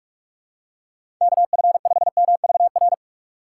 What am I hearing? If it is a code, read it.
Morse code, heard as a keyed tone, about 42 wpm, 700 Hz. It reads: KF5MVR
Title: USA Call Signs - 6 Characters - 4x Character Spacing - 45wpm